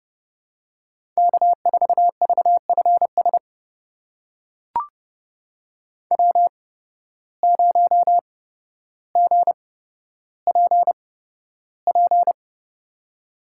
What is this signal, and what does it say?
Morse code, heard as a keyed tone, about 32 wpm, spaced as for 13 wpm, 700 Hz. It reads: K4VFS W0GPP
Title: USA Call Signs - 5 Characters - 8x Character Spacing - 30wpm